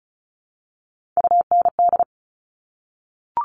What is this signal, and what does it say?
Morse code, heard as a keyed tone, about 36 wpm, 700 Hz. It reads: UND
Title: Top Three Letter Combinations in Words 35wpm